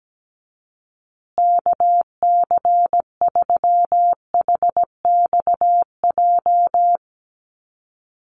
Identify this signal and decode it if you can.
Morse code, heard as a keyed tone, about 17 wpm, 700 Hz. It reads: KC3HXJ
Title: USA Call Signs Repeated 3x 17wpm